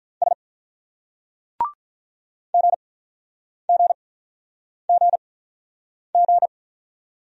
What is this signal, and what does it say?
Morse code, heard as a keyed tone, about 46 wpm, spaced as for 9 wpm, 700 Hz. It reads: S GGGG